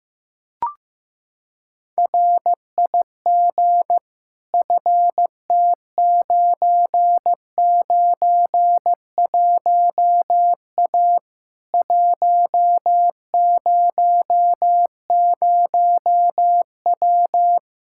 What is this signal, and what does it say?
Morse code, heard as a keyed tone, about 16 wpm, 700 Hz. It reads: RIG FT991A 100W
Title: Practice QSO - Exchange 2 - Info - 15wpm